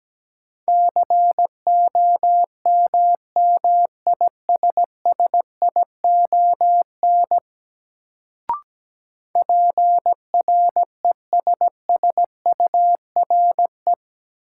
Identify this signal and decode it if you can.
Morse code, heard as a keyed tone, about 17 wpm, 700 Hz. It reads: COMMISSION PRESSURE
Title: Sentences from Top 1400 Words - New Word Drill Alt - 17wpm